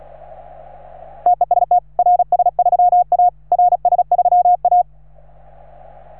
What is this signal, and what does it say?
Morse code, heard as a keyed tone, about 36 wpm, 710 Hz. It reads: TEST RS3A RS3A
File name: RS3A-20